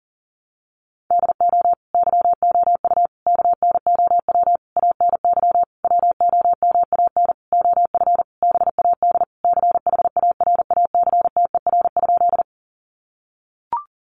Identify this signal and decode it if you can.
Morse code, heard as a keyed tone, about 40 wpm, 700 Hz. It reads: DO YOU KNOW ANY WOMAN OF BAD CHARACTER?
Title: Sentences from Top 1200 Words 40wpm